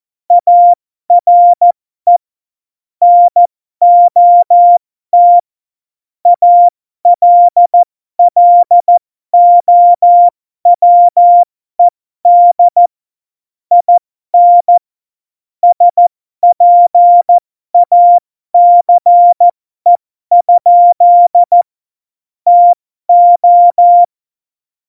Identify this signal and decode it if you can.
Morse code, heard as a keyed tone, about 14 wpm, spaced as for 12 wpm, 700 Hz. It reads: ARE NOT ALLOWED IN SPACE? TO
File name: Morsekurs / Witze